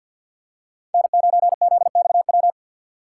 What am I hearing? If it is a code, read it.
Morse code, heard as a keyed tone, about 50 wpm, 700 Hz. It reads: N9ZXW
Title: USA Call Signs - 5 Characters - 2x Character Spacing - 50wpm